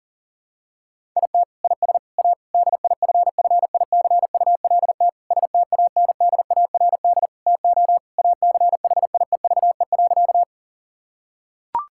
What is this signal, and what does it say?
Morse code, heard as a keyed tone, about 40 wpm, 700 Hz. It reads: IT IS A DIFFICULT STANDARD TO ACHIEVE.
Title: Sentences from Top 1000 Words - Review Part 6 - 40wpm